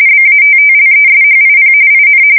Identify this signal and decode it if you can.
RTTY, about 45.45 baud. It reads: BATTLESHIP IN
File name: fsk170 / USS-New-Jersey